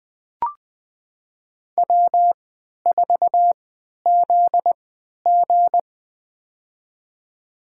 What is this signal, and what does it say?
Morse code, heard as a keyed tone, about 20 wpm, spaced as for 11 wpm, 700 Hz. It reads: W4ZG
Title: USA Call Signs - 4 Characters - 3x Character Spacing - 20wpm